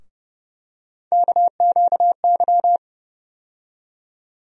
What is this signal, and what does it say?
Morse code, heard as a keyed tone, about 30 wpm, 700 Hz. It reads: KQY
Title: USA Call Sign Postfixes - 3 Letters 30wpm